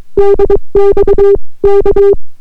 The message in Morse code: DX K